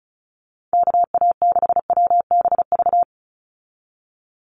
Morse code, heard as a keyed tone, about 36 wpm, 700 Hz. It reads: KA6WBV
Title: USA Call Signs 35wpm